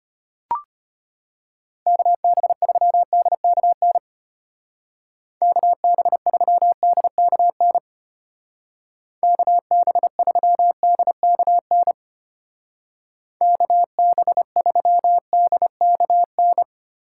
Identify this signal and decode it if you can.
Morse code, heard as a keyed tone, about 31 wpm, 700 Hz. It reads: KB3DKN KB3DKN KB3DKN KB3DKN
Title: Speed Racing - USA 2x3 Call Signs 25wpm